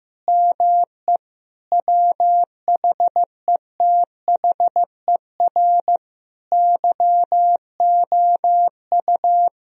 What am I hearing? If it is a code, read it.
Morse code, heard as a keyed tone, about 15 wpm, 700 Hz. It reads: ME WHETHER YOU
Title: Sentences from Top 1000 Words - Review Part 3 - 15wpm